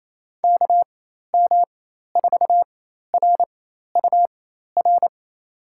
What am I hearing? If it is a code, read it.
Morse code, heard as a keyed tone, about 28 wpm, spaced as for 13 wpm, 700 Hz. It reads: KM4RUR
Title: USA Call Signs - 6 Characters - 4x Character Spacing - 28wpm